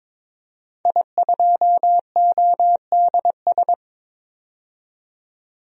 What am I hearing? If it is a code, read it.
Morse code, heard as a keyed tone, about 22 wpm, 700 Hz. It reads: I2ODS
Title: DXCC-100 Call Signs - 5 Characters - 22wpm